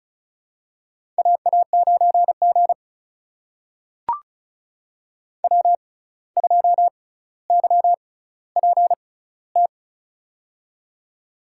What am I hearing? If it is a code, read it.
Morse code, heard as a keyed tone, about 36 wpm, spaced as for 17 wpm, 700 Hz. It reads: AA9G W2YPT
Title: USA Call Signs - 6x Character Spacing - 35wpm